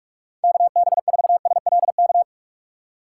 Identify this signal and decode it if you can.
Morse code, heard as a keyed tone, about 46 wpm, 700 Hz. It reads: KB4SLK
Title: USA Call Signs - 6x Character Spacing - 45wpm